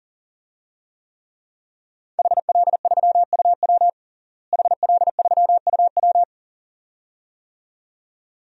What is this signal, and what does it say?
Morse code, heard as a keyed tone, about 40 wpm, 700 Hz. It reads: HL3UW HL3UW
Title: DXCC-100 Call Signs - 5 Characters - Repeated 2x - 40wpm